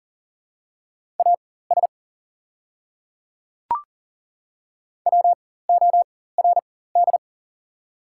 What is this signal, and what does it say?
Morse code, heard as a keyed tone, about 42 wpm, spaced as for 19 wpm, 690 Hz. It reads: AS WORD
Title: 4x Character Spacing - Top 100 Words - 40wpm